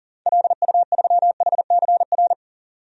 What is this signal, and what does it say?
Morse code, heard as a keyed tone, about 39 wpm, 700 Hz. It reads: LU3HCR
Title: DXCC-100 Call Signs - 6 Characters - 40wpm